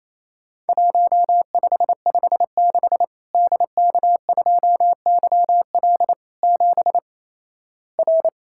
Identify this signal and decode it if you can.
Morse code, heard as a keyed tone, about 28 wpm, 700 Hz. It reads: 1556 DK2YL 7 R
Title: Worked All Europe DX Contest - QTC Exchange - 28wpm